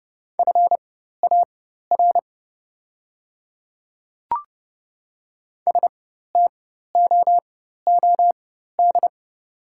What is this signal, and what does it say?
Morse code, heard as a keyed tone, about 31 wpm, spaced as for 14 wpm, 690 Hz. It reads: FAR STOOD
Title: 4x Character Spacing - Top 500 Words - 30wpm